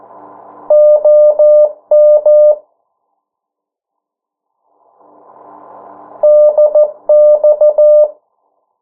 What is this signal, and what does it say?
Morse code, heard as a keyed tone, about 14 wpm, 600 Hz. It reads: OM DX
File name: morse mp3 / 09 urok 8